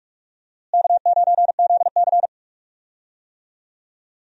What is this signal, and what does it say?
Morse code, heard as a keyed tone, about 46 wpm, 700 Hz. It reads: K9ZC
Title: USA Call Signs - 4 Characters - Repeated 3x - 45wpm